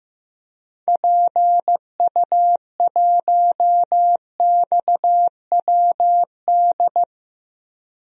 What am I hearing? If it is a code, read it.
Morse code, heard as a keyed tone, about 15 wpm, 700 Hz. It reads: PU1XWD
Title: DXCC-100 Call Signs - 6 Characters - Repeated 2x - 15wpm